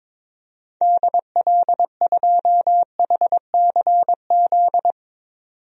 Morse code, heard as a keyed tone, about 22 wpm, 700 Hz. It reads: DL2HCZ
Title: DXCC-100 Call Signs - 6 Characters - 22wpm